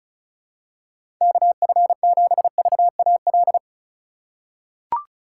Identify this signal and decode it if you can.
Morse code, heard as a keyed tone, about 36 wpm, 700 Hz. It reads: KF7VAL
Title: USA Call Signs - 6 Characters - 4x Character Spacing - 35wpm